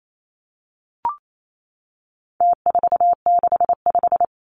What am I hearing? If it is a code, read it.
Morse code, heard as a keyed tone, about 28 wpm, 700 Hz. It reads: T465
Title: Numbers - Contest Style - 4-Digit 28wpm